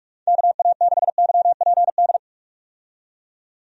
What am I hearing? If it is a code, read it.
Morse code, heard as a keyed tone, about 45 wpm, 700 Hz. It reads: KA6YPD
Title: USA Call Signs - 6 Characters - 45wpm